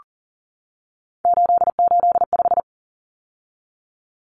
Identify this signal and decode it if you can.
Morse code, heard as a keyed tone, about 40 wpm, 700 Hz. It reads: 885
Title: Numbers - 3-Digit 40wpm